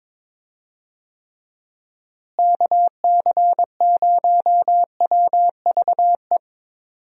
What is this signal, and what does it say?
Morse code, heard as a keyed tone, about 22 wpm, 700 Hz. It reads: KC0WVE
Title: USA Call Signs - 6 Characters - Repeated 2x - 22wpm